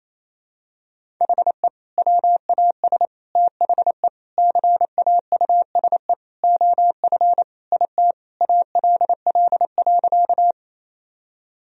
Morse code, heard as a keyed tone, about 28 wpm, 700 Hz. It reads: HE WAS THE CAUSE OF IT ALL.
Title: Sentences from Top 1000 Words - Review Part 1 - 28wpm